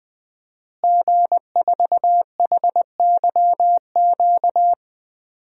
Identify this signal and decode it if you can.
Morse code, heard as a keyed tone, about 20 wpm, 700 Hz. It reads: G4HYQ
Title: DXCC-100 Call Signs - 5 Characters - Repeated 2x - 20wpm